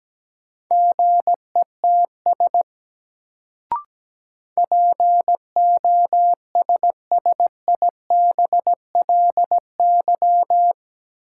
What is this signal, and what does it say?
Morse code, heard as a keyed tone, about 17 wpm, 700 Hz. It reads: GETS POSSIBLY
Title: Sentences from Top 1300 Words - New Word Drill - 17wpm